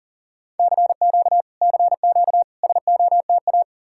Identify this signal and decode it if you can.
Morse code, heard as a keyed tone, about 52 wpm, 700 Hz. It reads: CQ CQ SOTA